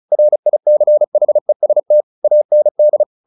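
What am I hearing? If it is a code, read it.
Morse code, heard as a keyed tone, about 34 wpm, 600 Hz. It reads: RICHEST AND
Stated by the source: Morse practice at 35 wpm from CW med Gnister, sending Ken Follett's novel Never in English.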